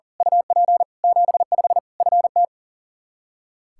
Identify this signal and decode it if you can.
Morse code, heard as a keyed tone, about 40 wpm, 700 Hz. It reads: UP 75 FT
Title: Practice QSO - Exchange 2 - Info - 40wpm